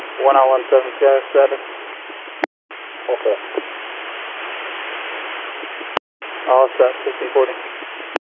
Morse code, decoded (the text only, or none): none